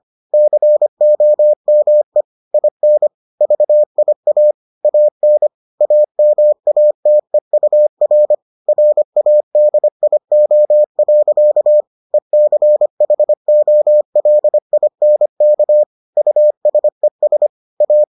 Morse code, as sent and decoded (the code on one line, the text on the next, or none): -.-. --- -- . / .. -. / ...- .. .- / .- -. / .- -- .- - . ..- .-. / .-. .- -.. .. --- .-.-.- / . -.-. .... --- .-.. .. -. -.- / ..- ... ... / .-
COME IN VIA AN AMATEUR RADIO. ECHOLINK USES A